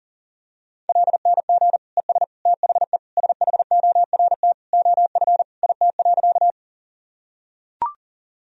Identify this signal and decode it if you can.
Morse code, heard as a keyed tone, about 40 wpm, 700 Hz. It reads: LNG ES THE SHORT OF IT.
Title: Sentences from Top 400 Words and Abbreviations 40wpm